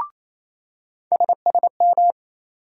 Morse code, heard as a keyed tone, about 28 wpm, 700 Hz. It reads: SSM
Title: USA Call Sign Postfixes - 3 Letters 28wpm